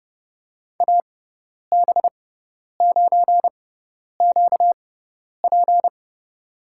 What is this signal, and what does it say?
Morse code, heard as a keyed tone, about 30 wpm, spaced as for 10 wpm, 700 Hz. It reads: AB9QP
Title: USA Call Signs - 5 Characters - 6x Character Spacing - 30wpm